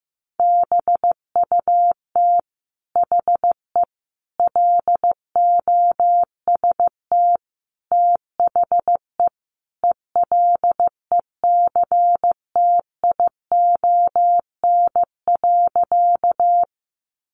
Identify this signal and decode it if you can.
Morse code, heard as a keyed tone, about 15 wpm, 700 Hz. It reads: BUT HE LOST THE ELECTION.
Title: Sentences from Top 1500 Words 15wpm